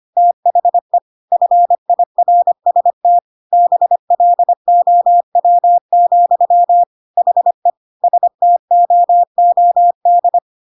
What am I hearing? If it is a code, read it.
Morse code, heard as a keyed tone, about 25 wpm, 700 Hz. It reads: THE FIRST BLOW, HE STOOD
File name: Pinocchio / Pinocchio ch1 25 25 ik1hge